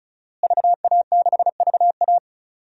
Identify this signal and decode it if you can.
Morse code, heard as a keyed tone, about 36 wpm, 700 Hz. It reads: VA6VA
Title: DXCC-100 Call Signs - 5 Characters - Repeated 3x - 35wpm